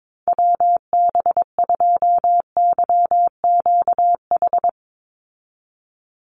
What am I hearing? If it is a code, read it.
Morse code, heard as a keyed tone, about 22 wpm, 690 Hz. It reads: WB2YQH
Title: Call Signs Repeated 2x - 22wpm